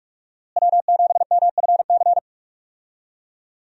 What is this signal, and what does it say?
Morse code, heard as a keyed tone, about 46 wpm, 700 Hz. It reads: W7MFC
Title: USA Call Signs - 5 Characters - Repeated 3x - 45wpm